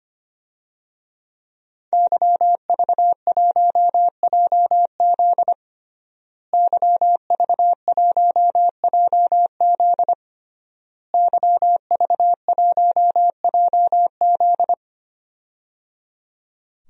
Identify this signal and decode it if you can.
Morse code, heard as a keyed tone, about 25 wpm, 700 Hz. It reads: YV1JZ YV1JZ YV1JZ
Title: DXCC-100 Call Signs - 5 Characters - Repeated 3x - 25wpm